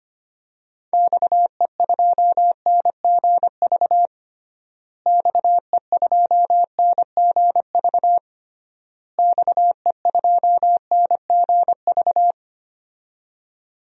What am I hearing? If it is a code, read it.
Morse code, heard as a keyed tone, about 25 wpm, 700 Hz. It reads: XE2NGV XE2NGV XE2NGV
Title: DXCC-100 Call Signs - 6 Characters - Repeated 3x - 25wpm